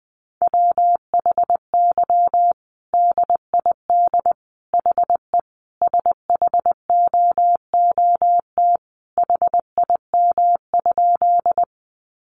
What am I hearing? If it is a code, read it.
Morse code, heard as a keyed tone, about 20 wpm, 700 Hz. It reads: WHY DID HE SHOOT HIM?
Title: Sentences from Top 1200 Words 20wpm